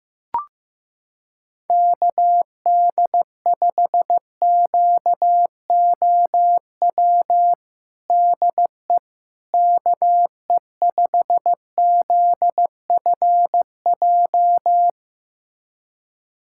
Morse code, heard as a keyed tone, about 15 wpm, 700 Hz. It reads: KD5QOW DE KE5ZFJ
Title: USA Callsign DE Callsign 15wpm